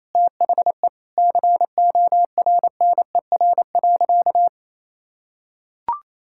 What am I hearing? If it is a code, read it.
Morse code, heard as a keyed tone, about 28 wpm, 700 Hz. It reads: THE CORNER.
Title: Sentences from Top 1600 Words 28wpm